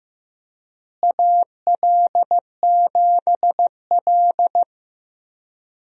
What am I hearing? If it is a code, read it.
Morse code, heard as a keyed tone, about 15 wpm, 700 Hz. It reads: AL7L
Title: USA Call Signs - 4 Characters - Repeated 3x - 15wpm